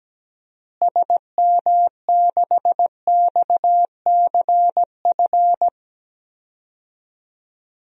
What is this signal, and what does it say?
Morse code, heard as a keyed tone, about 17 wpm, 700 Hz. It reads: SM6XCF
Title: DXCC-100 Call Signs - 6 Characters - Repeated 3x - 17wpm